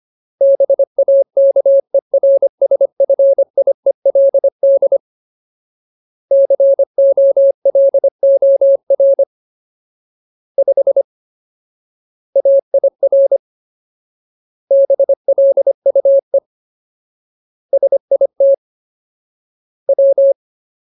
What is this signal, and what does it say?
Morse code, heard as a keyed tone, about 25 wpm, 550 Hz. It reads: BAKERSFIELD COLOR 5 AIR BLUE SIT W
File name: Ultimate Word List / 25WPM Ultimate